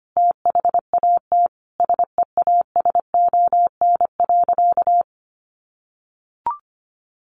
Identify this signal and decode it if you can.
Morse code, heard as a keyed tone, about 25 wpm, 700 Hz. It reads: THAT SEASON.
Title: Sentences from Top 1400 Words 25wpm